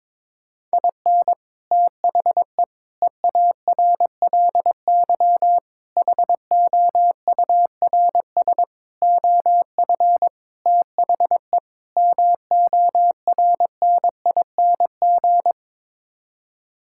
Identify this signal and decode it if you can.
Morse code, heard as a keyed tone, about 22 wpm, 700 Hz. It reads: IN THE EARLY HOURS OF THE MORNING
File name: practice / 22wpm / Sets of 7 Words - 22wpm